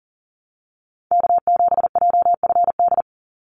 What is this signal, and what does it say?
Morse code, heard as a keyed tone, about 40 wpm, 700 Hz. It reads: K7JFD
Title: NA SOTA Activator Callsigns 40wpm